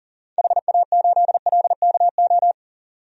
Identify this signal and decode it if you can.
Morse code, heard as a keyed tone, about 40 wpm, 700 Hz. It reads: HA8LKO
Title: DXCC-100 Call Signs - 6 Characters - 40wpm